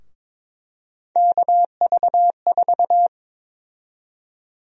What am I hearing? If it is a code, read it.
Morse code, heard as a keyed tone, about 22 wpm, 700 Hz. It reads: KV4
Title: USA Call Sign Prefixes - Repeated 2x - 22wpm